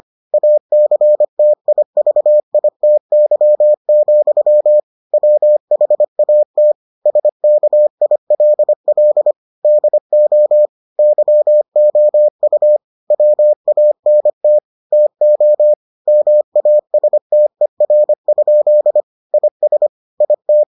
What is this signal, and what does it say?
Morse code, heard as a keyed tone, about 25 wpm, 600 Hz. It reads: ACTIVITY, WHAT SKILL DO YOU WANT TO MASTER? IS IT